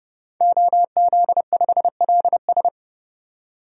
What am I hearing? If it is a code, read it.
Morse code, heard as a keyed tone, about 30 wpm, 700 Hz. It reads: OZ5LS
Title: DXCC-100 Call Signs - 5 Characters - 30wpm